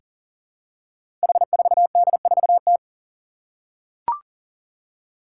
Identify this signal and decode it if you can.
Morse code, heard as a keyed tone, about 40 wpm, 700 Hz. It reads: H4D4T
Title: Five Challenging Letter-Number Combinations 40wpm